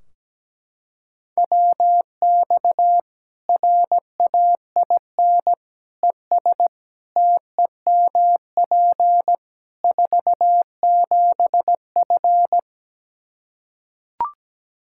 Morse code, heard as a keyed tone, about 17 wpm, 700 Hz. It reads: WX RAIN ES TEMP 47F
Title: Practice QSO - Exchange 2 - Info - 17wpm